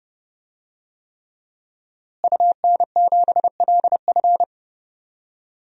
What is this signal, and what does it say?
Morse code, heard as a keyed tone, about 30 wpm, 700 Hz. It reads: UN7LF
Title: DXCC-100 Call Signs - 5 Characters - Repeated 3x - 30wpm